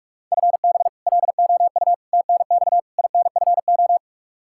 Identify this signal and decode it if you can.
Morse code, heard as a keyed tone, about 43 wpm, 700 Hz. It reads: FB LOU TNX INFO
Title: K1USN SST QSO Practice 45wpm